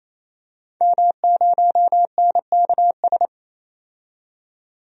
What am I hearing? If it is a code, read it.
Morse code, heard as a keyed tone, about 28 wpm, 700 Hz. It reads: M0NKS
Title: DXCC-100 Call Signs - 5 Characters - 28wpm